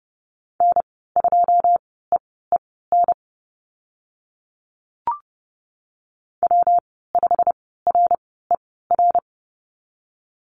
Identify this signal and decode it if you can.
Morse code, heard as a keyed tone, about 30 wpm, spaced as for 17 wpm, 700 Hz. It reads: N2EEN W5RER
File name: Lesson 19g - Letter w - Call Signs 30wpm